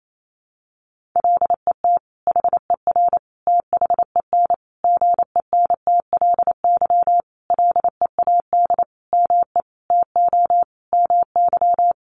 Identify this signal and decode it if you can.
Morse code, heard as a keyed tone, about 28 wpm, 700 Hz. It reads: LET HER THEN GENTLY LEAD ME TO MY
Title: Sentences from Top 1500 Words 28wpm